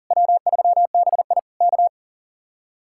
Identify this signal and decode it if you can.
Morse code, heard as a keyed tone, about 52 wpm, 700 Hz. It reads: W3BI K